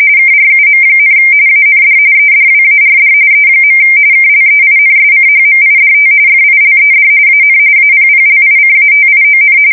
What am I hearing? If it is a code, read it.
RTTY, about 45.45 baud. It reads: KRUM, AND EDWARD E. KLEINSCHMIDT FILED AN APPLICATION
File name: fsk170 / teletype-corp